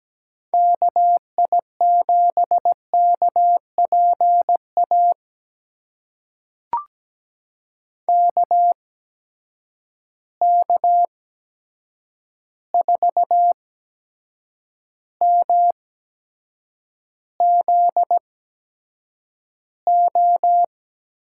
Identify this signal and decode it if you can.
Morse code, heard as a keyed tone, about 17 wpm, spaced as for 7 wpm, 700 Hz. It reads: KI7KPA KK4MZO
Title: USA Call Signs - 6 Characters - 8x Character Spacing - 17wpm